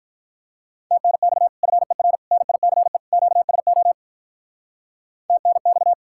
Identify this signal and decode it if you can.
Morse code, heard as a keyed tone, about 45 wpm, 700 Hz. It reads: TNX FER NICE QSO TNX